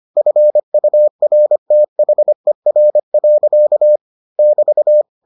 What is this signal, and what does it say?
Morse code, heard as a keyed tone, about 25 wpm, 600 Hz. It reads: FURTHER. <BT>